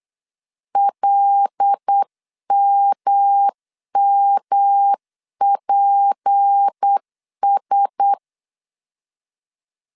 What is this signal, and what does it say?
Morse code, heard as a keyed tone, about 8 wpm, 800 Hz. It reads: LMMPS
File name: CW / DET / 1 6 1500C S C 8wpm 5219